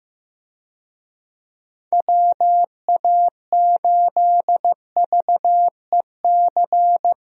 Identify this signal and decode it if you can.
Morse code, heard as a keyed tone, about 15 wpm, 700 Hz. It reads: WA8VEC
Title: USA Call Signs - 6 Characters - Repeated 2x - 15wpm